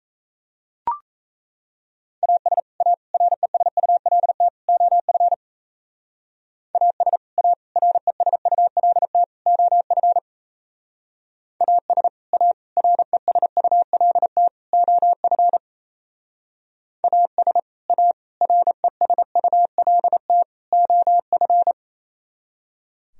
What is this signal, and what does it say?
Morse code, heard as a keyed tone, about 34 wpm, 700 Hz. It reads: AS A RESULT OF AS A RESULT OF AS A RESULT OF AS A RESULT OF
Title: Speed-Racing - Sets of 4 Words 28wpm